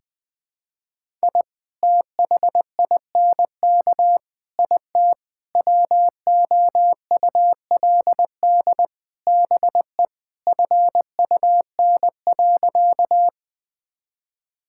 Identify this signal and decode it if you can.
Morse code, heard as a keyed tone, about 21 wpm, 700 Hz. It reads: I THINK IT WOULD BE FUN.
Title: Sentences with 5-Word Phrases - 20wpm